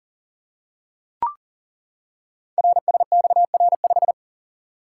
Morse code, heard as a keyed tone, about 38 wpm, 690 Hz. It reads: RSXR5
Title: Code Groups - 5 Letter:Numbers - 40wpm